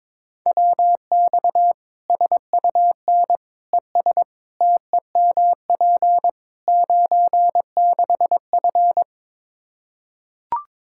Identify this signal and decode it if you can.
Morse code, heard as a keyed tone, about 22 wpm, 700 Hz. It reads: WX SUN ES TEMP 96F
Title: Practice QSO - Exchange 2 - Info - 22wpm